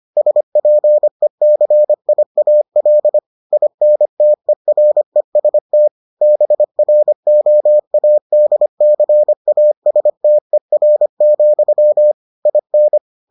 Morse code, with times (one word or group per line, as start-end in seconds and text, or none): 0.2-3.2 SPECIAL
3.5-5.9 INTEREST
6.2-12.1 BROADCASTER,
12.5-13.0 IN